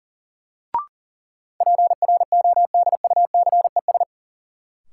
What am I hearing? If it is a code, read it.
Morse code, heard as a keyed tone, about 40 wpm, 700 Hz. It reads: PRODUCES
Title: Sentences from Top 1000 Words - Review Part 6 - New Word Drill - 40wpm